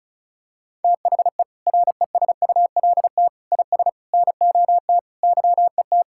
Morse code, heard as a keyed tone, about 36 wpm, 700 Hz. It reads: THE RESULT IS NOT YET
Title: Sentences from Top 1000 Words - Review Part 5 - 35wpm